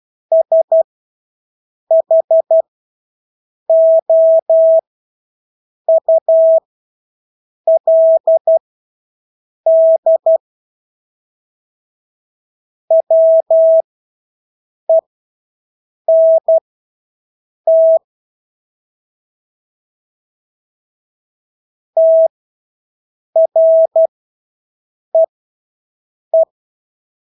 Morse code, heard as a keyed tone, about 12 wpm, spaced as for 6 wpm, 650 Hz. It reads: SHOULD WENT TREE
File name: Engelske ord 12-6 wpm